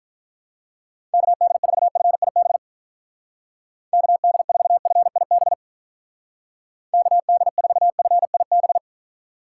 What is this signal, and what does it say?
Morse code, heard as a keyed tone, about 44 wpm, 700 Hz. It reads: KD4FIB KD4FIB KD4FIB